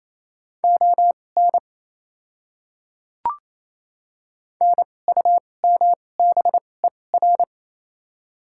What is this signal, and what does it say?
Morse code, heard as a keyed tone, about 29 wpm, spaced as for 21 wpm, 690 Hz. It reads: ON NUMBER
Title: 2x Character Spacing - Top 100 Words - 28wpm